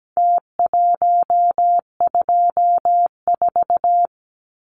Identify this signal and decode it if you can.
Morse code, heard as a keyed tone, about 17 wpm, 700 Hz. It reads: T124
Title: Numbers - Contest Style - 4-Digit 17wpm